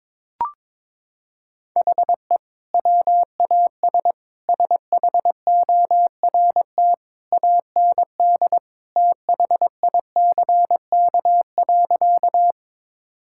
Morse code, heard as a keyed tone, about 22 wpm, 700 Hz. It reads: HE WAS SHORT AND THICK.
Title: Sentences from Top 1600 Words 22wpm